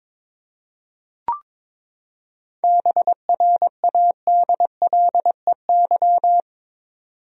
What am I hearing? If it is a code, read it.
Morse code, heard as a keyed tone, about 22 wpm, 700 Hz. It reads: BRADLEY
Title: USA Common Names - 22wpm